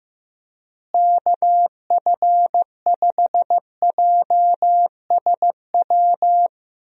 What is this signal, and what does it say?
Morse code, heard as a keyed tone, about 15 wpm, 700 Hz. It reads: KF5JSW
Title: USA Call Signs - 6 Characters - Repeated 3x - 15wpm